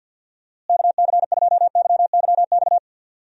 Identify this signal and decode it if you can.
Morse code, heard as a keyed tone, about 52 wpm, 700 Hz. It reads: KC2YYX
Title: USA Call Signs - 6 Characters - Repeated 2x - 50wpm